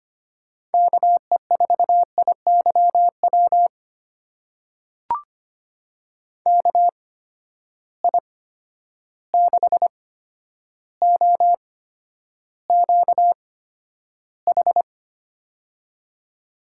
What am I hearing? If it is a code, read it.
Morse code, heard as a keyed tone, about 25 wpm, spaced as for 11 wpm, 700 Hz. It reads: KE4IYW KI6OQH
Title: USA Call Signs - 6 Characters - 8x Character Spacing - 25wpm